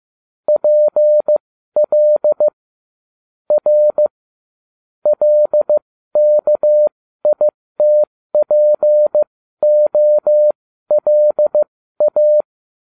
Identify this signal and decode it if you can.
Morse code, heard as a keyed tone, about 15 wpm, spaced as for 12 wpm, 600 Hz. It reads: PL R LKITPOLA